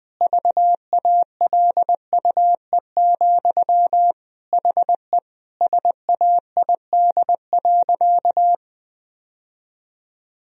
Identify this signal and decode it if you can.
Morse code, heard as a keyed tone, about 20 wpm, 700 Hz. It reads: VALUE, HE SAID.
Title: Sentences from Top 1000 Words - Review Part 3 - 20wpm